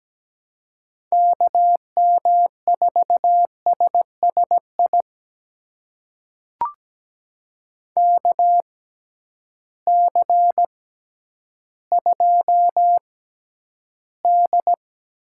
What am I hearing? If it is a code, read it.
Morse code, heard as a keyed tone, about 17 wpm, spaced as for 10 wpm, 700 Hz. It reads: KM4SSI KC2D
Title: USA Call Signs - 6 Characters - 6x Character Spacing - 17wpm